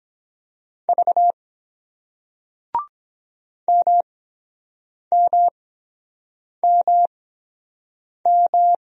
Morse code, heard as a keyed tone, about 22 wpm, spaced as for 7 wpm, 700 Hz. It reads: V MMMM